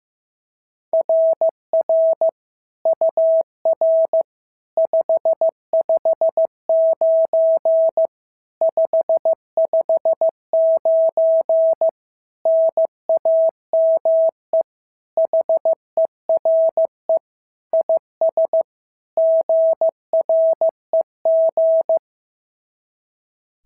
Morse code, heard as a keyed tone, about 15 wpm, 650 Hz. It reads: RR UR 559 559 NAME HERE IS GREG
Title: SOTA QSO Practice 15wpm